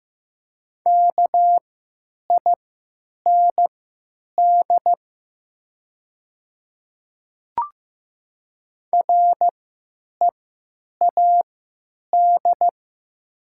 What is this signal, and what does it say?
Morse code, heard as a keyed tone, about 15 wpm, spaced as for 9 wpm, 700 Hz. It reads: KIND READ
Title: 3x Character Spacing - Top 200 Words - 15wpm